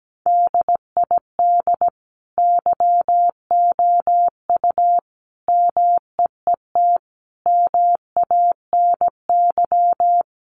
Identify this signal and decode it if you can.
Morse code, heard as a keyed tone, about 17 wpm, 700 Hz. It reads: DID YOU MEET MANY
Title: Sentences from Top 1100 Words 17wpm